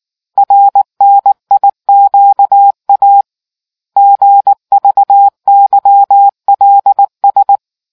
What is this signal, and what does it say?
Morse code, heard as a keyed tone, about 19 wpm, 800 Hz. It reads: RNIQA GVYLS